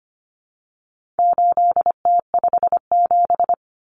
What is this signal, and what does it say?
Morse code, heard as a keyed tone, about 25 wpm, 700 Hz. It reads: 8T57
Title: Common Cut Numbers - 4-Digit 25wpm